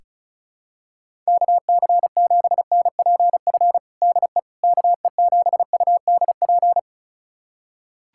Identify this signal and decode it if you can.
Morse code, heard as a keyed tone, about 34 wpm, 700 Hz. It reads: KC7NPF DE KE7UDP
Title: USA Callsign DE Callsign - 6 Characters - 35wpm